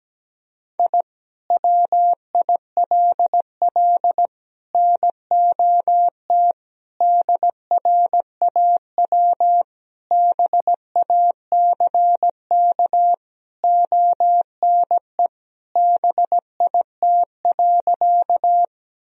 Morse code, heard as a keyed tone, about 17 wpm, 700 Hz. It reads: I WILL NOT DRAW BACK ONE BIT.
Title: Sentences from Top 1000 Words - Review Part 1 - 17wpm